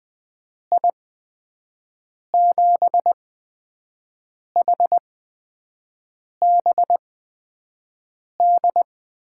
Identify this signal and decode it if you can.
Morse code, heard as a keyed tone, about 20 wpm, spaced as for 5 wpm, 700 Hz. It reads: I7HBD
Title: USA Call Signs - 6 Characters - 8x Character Spacing - 20wpm